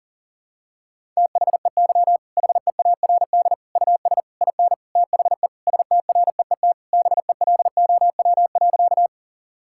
Morse code, heard as a keyed tone, about 40 wpm, 700 Hz. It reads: THEY HEARD US IN THE STREET BELOW.
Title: Sentences from Top 1000 Words - Review Part 5 - 40wpm